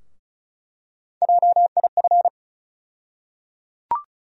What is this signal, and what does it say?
Morse code, heard as a keyed tone, about 36 wpm, 700 Hz. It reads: JIF
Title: USA Call Sign Postfixes - 3 Letters 35wpm